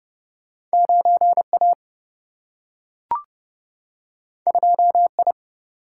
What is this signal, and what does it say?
Morse code, heard as a keyed tone, about 30 wpm, 700 Hz. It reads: 9 A 2 I
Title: Single-Letter Single-Number 30wpm